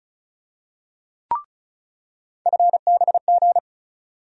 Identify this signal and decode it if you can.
Morse code, heard as a keyed tone, about 36 wpm, 700 Hz. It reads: FBG
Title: USA Call Sign Postfixes - 3 Letters 35wpm